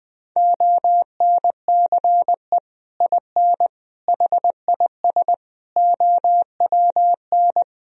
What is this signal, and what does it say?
Morse code, heard as a keyed tone, about 20 wpm, 700 Hz. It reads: ONCE IN HIS OWN
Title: Sentences from Top 1000 Words - Review Part 2 - 20wpm